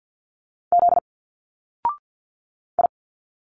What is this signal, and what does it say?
Morse code, heard as a keyed tone, about 50 wpm, 700 Hz. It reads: Z I
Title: Single Letters 50wpm